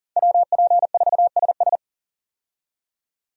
Morse code, heard as a keyed tone, about 40 wpm, 700 Hz. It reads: WP4SS
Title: USA Call Signs - 5 Characters - 40wpm